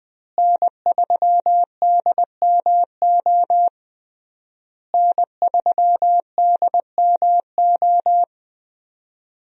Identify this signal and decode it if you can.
Morse code, heard as a keyed tone, about 20 wpm, 700 Hz. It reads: N3DMO N3DMO
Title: USA Call Signs - 5 Characters - Repeated 3x - 20wpm